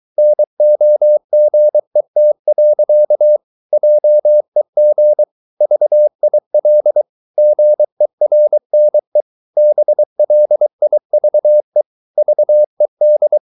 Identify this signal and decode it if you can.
Morse code, heard as a keyed tone, about 23 wpm, 600 Hz. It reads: NOGET. JEG VIL GERNE BLIVE VED
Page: Ferie 23 wpm